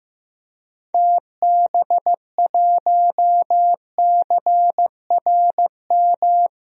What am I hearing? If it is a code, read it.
Morse code, heard as a keyed tone, about 15 wpm, 700 Hz. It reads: TB1CRM
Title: DXCC-100 Call Signs - 6 Characters - Repeated 3x - 15wpm